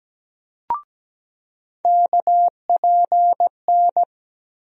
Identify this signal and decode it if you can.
Morse code, heard as a keyed tone, about 17 wpm, 700 Hz. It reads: KPN
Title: USA Call Sign Postfixes - 3 Letters 17wpm